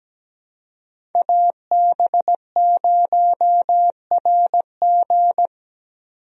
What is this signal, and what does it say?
Morse code, heard as a keyed tone, about 17 wpm, 700 Hz. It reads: AB0RG
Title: USA Call Signs - 5 Characters - 17wpm